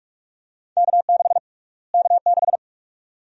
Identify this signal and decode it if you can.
Morse code, heard as a keyed tone, about 46 wpm, 700 Hz. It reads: K6 K6
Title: USA Call Sign Prefixes - Repeated 2x - 45wpm